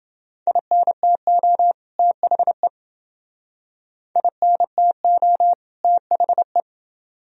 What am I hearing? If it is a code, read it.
Morse code, heard as a keyed tone, about 29 wpm, 700 Hz. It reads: INTO THE INTO THE